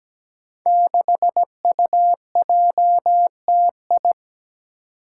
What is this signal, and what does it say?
Morse code, heard as a keyed tone, about 17 wpm, 700 Hz. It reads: 6UJTI
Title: Five Challenging Letter-Number Combinations 17wpm